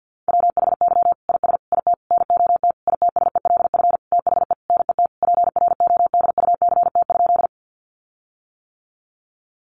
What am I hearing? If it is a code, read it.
Morse code, heard as a keyed tone, about 50 wpm, 700 Hz. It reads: WHY IS IT NOT ITSELF THE NET PRODUCT?
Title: Sentences from Top 1400 Words 50wpm